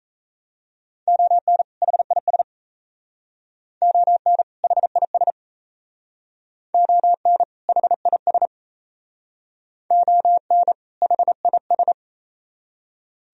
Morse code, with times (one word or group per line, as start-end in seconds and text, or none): 1.1-1.6 ON
1.8-2.4 HIS
3.8-4.4 ON
4.6-5.3 HIS
6.7-7.4 ON
7.7-8.5 HIS
9.9-10.7 ON
11.0-11.9 HIS